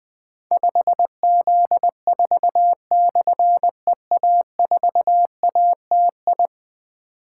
Morse code, heard as a keyed tone, about 21 wpm, 700 Hz. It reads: 5Z4/EA4ATI
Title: Difficult Call Signs 20wpm